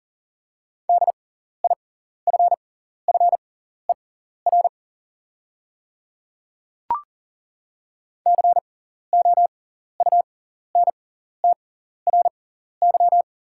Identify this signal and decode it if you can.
Morse code, heard as a keyed tone, about 41 wpm, spaced as for 14 wpm, 700 Hz. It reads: DIFFER COUNTRY
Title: 6x Character Spacing - Top 500 Words - 40wpm